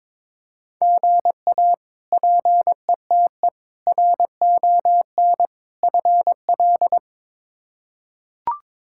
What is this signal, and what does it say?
Morse code, heard as a keyed tone, about 22 wpm, 700 Hz. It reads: GA PETE RON FL
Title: K1USN SST QSO Practice - 22wpm